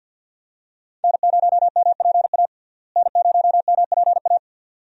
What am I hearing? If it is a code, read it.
Morse code, heard as a keyed tone, about 50 wpm, 700 Hz. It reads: N0MPA N0MPA
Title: USA Call Signs - 5 Characters - Repeated 2x - 50wpm